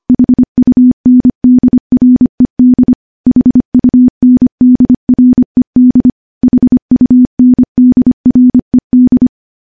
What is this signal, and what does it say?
Morse code, heard as a keyed tone, about 25 wpm, 250 Hz. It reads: HUNDRED HUNDRED HUNDRED
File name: morse / morse-training-25WPM